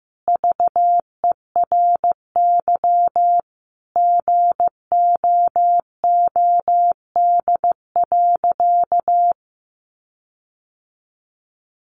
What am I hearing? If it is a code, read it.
Morse code, heard as a keyed tone, about 15 wpm, 700 Hz. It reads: VERY GOOD.
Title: Sentences from Top 1400 Words 15wpm